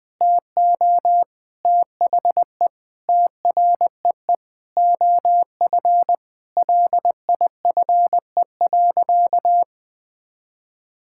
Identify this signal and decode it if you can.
Morse code, heard as a keyed tone, about 21 wpm, 700 Hz. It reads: TO THE TREE OF LIFE.
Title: Sentences from Top 1000 Words - Review Part 6 - 20wpm